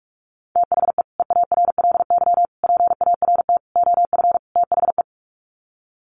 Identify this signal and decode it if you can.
Morse code, heard as a keyed tone, about 44 wpm, 690 Hz. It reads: THE EARLY PART OF THE
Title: Sets of 5 Words Encore2 45wpm